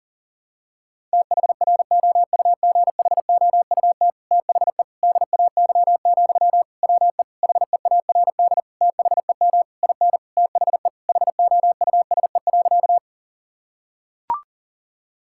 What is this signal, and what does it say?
Morse code, heard as a keyed tone, about 40 wpm, 700 Hz. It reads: THROUGHOUT THE DAY, WE HEARD THEM IN THE HOUSE.
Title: Sentences from Top 1000 Words - Review Part 5 - 40wpm